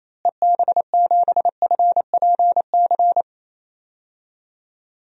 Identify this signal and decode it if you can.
Morse code, heard as a keyed tone, about 28 wpm, 700 Hz. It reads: EB7FPC
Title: DXCC-100 Call Signs - 6 Characters - Repeated 3x - 28wpm